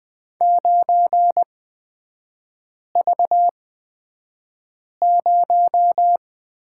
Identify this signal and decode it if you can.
Morse code, heard as a keyed tone, about 20 wpm, spaced as for 5 wpm, 700 Hz. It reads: 9V0
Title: 229 Lesson 37e - Letter x - Three Characters 20wpm